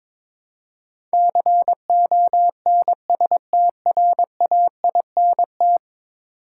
Morse code, heard as a keyed tone, about 22 wpm, 700 Hz. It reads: CONSTRAINT
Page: morse-code-ninja-practice-22wpm